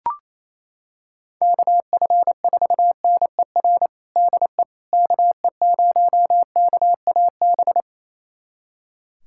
Morse code, heard as a keyed tone, about 28 wpm, 700 Hz. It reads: KF4NER DE KE0KAB
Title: USA Callsign DE Callsign 28wpm